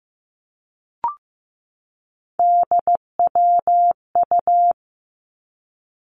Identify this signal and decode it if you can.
Morse code, heard as a keyed tone, about 15 wpm, 700 Hz. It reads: DWU
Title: Three Challenging Letter-Number Combinations 15wpm